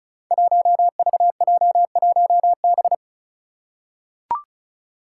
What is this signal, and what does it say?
Morse code, heard as a keyed tone, about 34 wpm, 700 Hz. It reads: 1VJ1B
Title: Five Challenging Letter-Number Combinations 35wpm